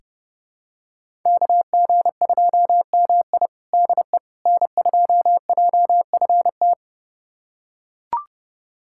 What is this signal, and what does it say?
Morse code, heard as a keyed tone, about 30 wpm, 700 Hz. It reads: KG2MI DE N2JFT
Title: USA Callsign DE Callsign 30wpm